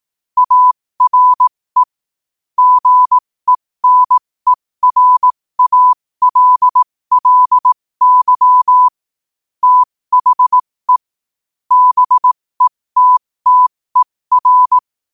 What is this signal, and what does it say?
Morse code, heard as a keyed tone, about 18 wpm, spaced as for 15 wpm, 1000 Hz. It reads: ARE GENERALLY THE BETTER
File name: qotdfiles / 15wpm